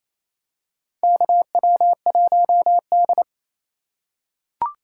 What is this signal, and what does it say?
Morse code, heard as a keyed tone, about 28 wpm, 700 Hz. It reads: KW1D
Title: USA Call Signs - 4 Characters - 3x Character Spacing - 28wpm